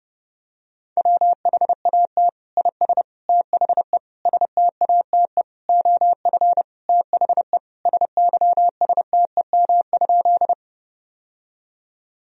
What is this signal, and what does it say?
Morse code, heard as a keyed tone, about 32 wpm, 700 Hz. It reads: WHAT IS THE STATE OF THE SYSTEM?
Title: Sentences from Top 1000 Words - Review Part 3 - 30wpm